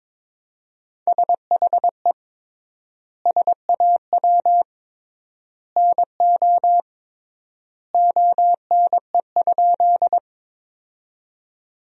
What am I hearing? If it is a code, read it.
Morse code, heard as a keyed tone, about 22 wpm, 700 Hz. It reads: SHE SAW NO ONE?
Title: 3x Word-Spacing - Sentences from Top 300 Words - 22wpm